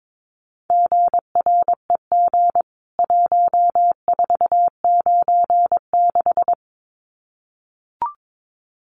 Morse code, heard as a keyed tone, about 22 wpm, 700 Hz. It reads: GREG 1496
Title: CWT Exchange 22wpm